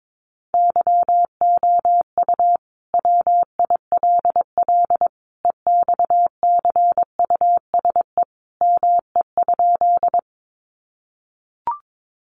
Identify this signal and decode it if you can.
Morse code, heard as a keyed tone, about 22 wpm, 700 Hz. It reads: YOU WILL EXCUSE ME?
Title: Sentences from Top 700 Words 22wpm